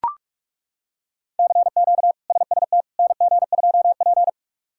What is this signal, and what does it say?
Morse code, heard as a keyed tone, about 42 wpm, 700 Hz. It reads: CQ SST NG2P
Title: K1USN SST QSO Practice - 45wpm